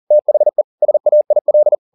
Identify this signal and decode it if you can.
Morse code, heard as a keyed tone, about 40 wpm, 600 Hz. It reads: THE SAIL
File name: the old man and the sea 2 40wpm